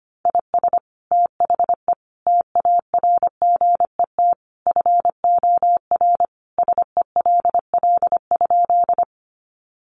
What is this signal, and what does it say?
Morse code, heard as a keyed tone, about 25 wpm, 700 Hz. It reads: IS THE TARGET FOR SELL?
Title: Sentences from Top 1200 Words 25wpm